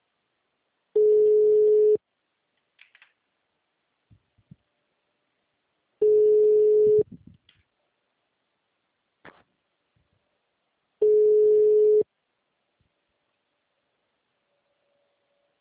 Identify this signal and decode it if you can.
Ringback tone (Europe, 425 Hz).
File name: out- -2011- - - .3893